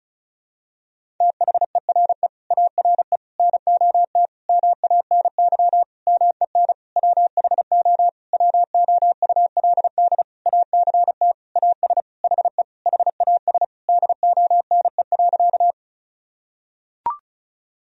Morse code, heard as a keyed tone, about 34 wpm, 700 Hz. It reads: THERE ARE NOT MANY MEN WHO WOULD ACT AS HE HAS DONE.
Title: Sentences from Top 1000 Words - Review Part 1 - 35wpm